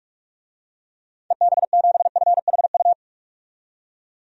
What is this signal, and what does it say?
Morse code, heard as a keyed tone, about 44 wpm, 700 Hz. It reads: EB7FHU
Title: DXCC-100 Call Signs - 6 Characters - 45wpm